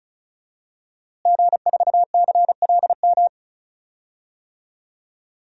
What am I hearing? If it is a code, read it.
Morse code, heard as a keyed tone, about 34 wpm, 700 Hz. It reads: G4CLM
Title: DXCC-100 Call Signs - 5 Characters - Repeated 3x - 35wpm